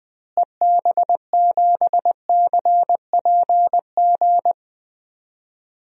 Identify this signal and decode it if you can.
Morse code, heard as a keyed tone, about 21 wpm, 700 Hz. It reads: EB7CPG
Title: DXCC-100 Call Signs - 6 Characters - Repeated 2x - 20wpm